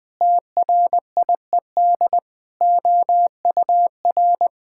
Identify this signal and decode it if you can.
Morse code, heard as a keyed tone, about 20 wpm, 700 Hz. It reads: TRIED OUR
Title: Sentences with 7-Word Phrases - 20wpm